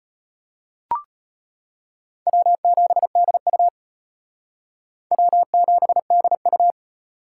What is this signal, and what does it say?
Morse code, heard as a keyed tone, about 32 wpm, 700 Hz. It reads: W7DU W7DU